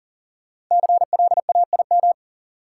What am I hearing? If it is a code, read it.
Morse code, heard as a keyed tone, about 37 wpm, 700 Hz. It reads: CLAIM
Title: Sentences from Top 1500 Words - New Word Drill Alt - 40wpm